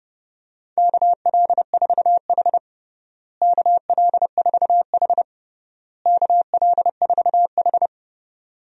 Morse code, heard as a keyed tone, about 30 wpm, 700 Hz. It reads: KL4H KL4H KL4H
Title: USA Call Signs - 4 Characters - Repeated 3x - 30wpm